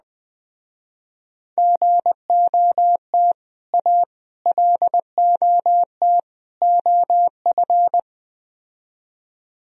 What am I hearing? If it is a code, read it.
Morse code, heard as a keyed tone, about 21 wpm, 700 Hz. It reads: GOT A LOT OF